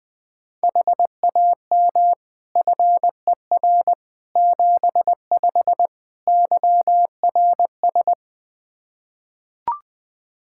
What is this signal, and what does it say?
Morse code, heard as a keyed tone, about 20 wpm, 700 Hz. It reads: HAM FER 75 YRS
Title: K1USN SST QSO Practice 20wpm